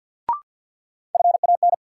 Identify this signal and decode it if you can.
Morse code, heard as a keyed tone, about 52 wpm, 700 Hz. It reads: FAN
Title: Sentences from Top 1300 Words - New Word Drill Alt - 50wpm